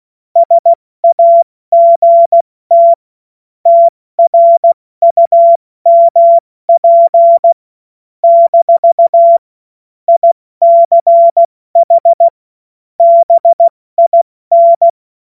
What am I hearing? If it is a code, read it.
Morse code, heard as a keyed tone, about 16 wpm, spaced as for 14 wpm, 680 Hz. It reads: SAGT TRUMP - ICH BIN
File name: Morsekurs / Witze / Witz 16. Februar 2021, 70 F80